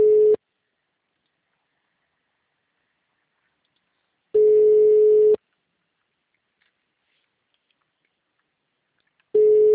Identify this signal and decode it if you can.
Ringback tone (Europe, 425 Hz).
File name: out- -2046- - - .3120